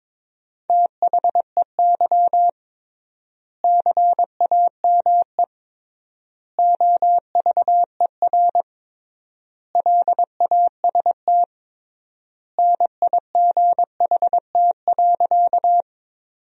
Morse code, heard as a keyed tone, about 22 wpm, 700 Hz. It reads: THEY CAME OVER LAST NIGHT.
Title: 3x Word-Spacing - Sentences from Top 300 Words - 22wpm